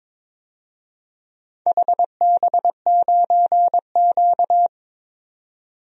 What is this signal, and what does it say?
Morse code, heard as a keyed tone, about 23 wpm, 700 Hz. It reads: HB9Q
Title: DXCC-100 Call Signs - 4 Characters - 22wpm